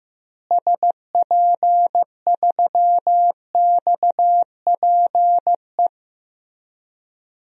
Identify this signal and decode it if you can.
Morse code, heard as a keyed tone, about 15 wpm, 700 Hz. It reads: SP3XPE
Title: DXCC-100 Call Signs - 6 Characters - Repeated 2x - 15wpm